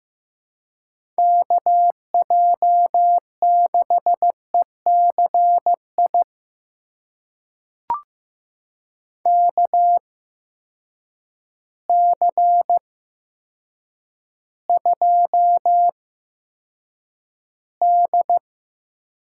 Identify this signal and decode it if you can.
Morse code, heard as a keyed tone, about 15 wpm, spaced as for 8 wpm, 700 Hz. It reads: KJ6ECI KC2D
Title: USA Call Signs - 6 Characters - 8x Character Spacing - 15wpm